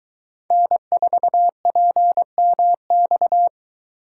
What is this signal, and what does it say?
Morse code, heard as a keyed tone, about 23 wpm, 700 Hz. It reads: N4PMX